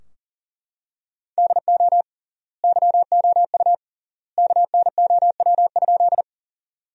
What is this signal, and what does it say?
Morse code, heard as a keyed tone, about 40 wpm, 700 Hz. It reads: DO YOU KNOW?
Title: 3x Word-Spacing - Sentences from Top 100 Words - 40wpm